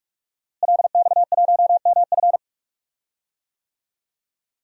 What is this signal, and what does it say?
Morse code, heard as a keyed tone, about 43 wpm, 700 Hz. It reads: LX1MF
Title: DXCC-100 Call Signs - 5 Characters - Repeated 2x - 45wpm